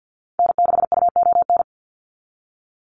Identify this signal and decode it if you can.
Morse code, heard as a keyed tone, about 50 wpm, 700 Hz. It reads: N6UON
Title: USA Call Signs 50wpm